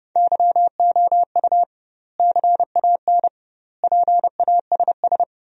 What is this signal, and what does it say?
Morse code, heard as a keyed tone, about 30 wpm, 700 Hz. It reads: YOU CAN PASS
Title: 2x Word-Spacing - Sentences from Top 400 Words - 30wpm